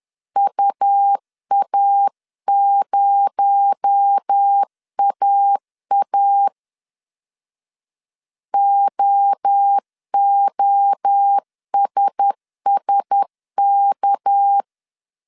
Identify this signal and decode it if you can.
Morse code, heard as a keyed tone, about 10 wpm, 800 Hz. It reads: UA0AA OOSSK